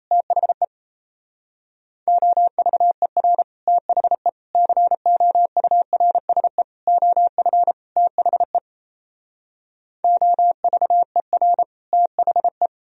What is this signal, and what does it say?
Morse code, heard as a keyed tone, about 37 wpm, 700 Hz. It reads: THE OVER THE COURSE OF THE OVER THE